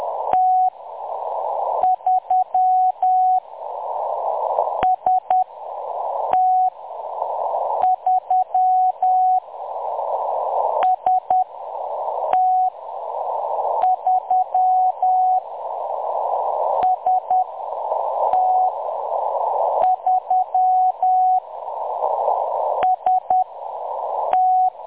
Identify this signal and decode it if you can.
Morse code, heard as a keyed tone, about 10 wpm, spaced as for 6 wpm, 740 Hz. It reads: T3 ST3 ST3 ST3 ST
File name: M23-ST3- -0858utc-10184khz